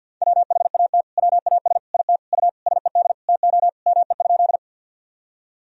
Morse code, heard as a keyed tone, about 45 wpm, 700 Hz. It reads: WHAT WAS IT U SED TO ME?
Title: Sentences from Top 200 Words and Abbreviations 50wpm